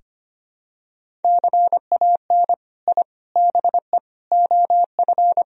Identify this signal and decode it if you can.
Morse code, heard as a keyed tone, about 25 wpm, 700 Hz. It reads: CAN I BE OF
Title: Sentences from Top 1000 Words - Review Part 4 - 25wpm